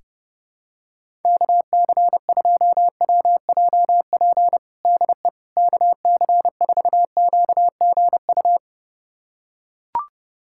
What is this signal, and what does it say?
Morse code, heard as a keyed tone, about 30 wpm, 700 Hz. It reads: KC2WJP DE KC4QGU
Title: USA Callsign DE Callsign - 6 Characters - 30wpm